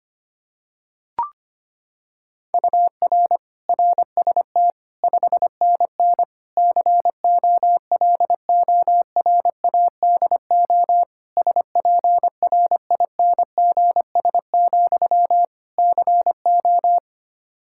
Morse code, heard as a keyed tone, about 25 wpm, 700 Hz. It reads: UR RST 5NN COLORADO SPRINGS, CO
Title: Practice QSO - Exchange 1 - Beginning - 25wpm